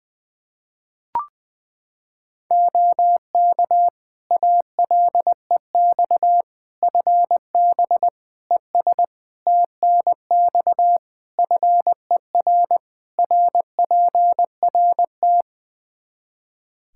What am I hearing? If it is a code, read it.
Morse code, heard as a keyed tone, about 21 wpm, 700 Hz. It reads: OK ALEX FB ES TNX FER RPRT
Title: Practice QSO - Exchange 2 - Info - 20wpm